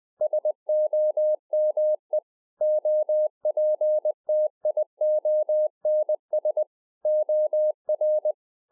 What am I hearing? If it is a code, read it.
Morse code, heard as a keyed tone, about 20 wpm, 610 Hz. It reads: SOME OPTIONS OR
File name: 2011-04-mp3 / A SMALL TRANCEIVER 3 20WPM